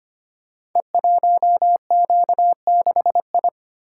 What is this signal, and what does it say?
Morse code, heard as a keyed tone, about 25 wpm, 700 Hz. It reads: E1Q6I
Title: Code Groups - 5 Letter:Numbers - 25wpm